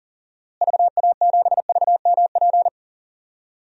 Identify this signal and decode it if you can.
Morse code, heard as a keyed tone, about 40 wpm, 700 Hz. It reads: VA7VMP
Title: DXCC-100 Call Signs - 6 Characters - Repeated 2x - 40wpm